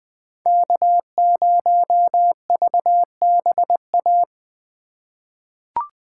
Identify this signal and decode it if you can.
Morse code, heard as a keyed tone, about 20 wpm, 700 Hz. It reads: K0VBA
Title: USA Call Signs - 5 Characters - 3x Character Spacing - 20wpm